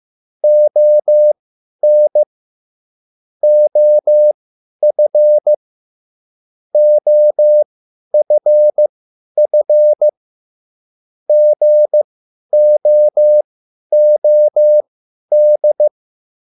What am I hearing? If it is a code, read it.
Morse code, heard as a keyed tone, about 15 wpm, spaced as for 10 wpm, 600 Hz. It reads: ON OF OFF GOOD
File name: cwbook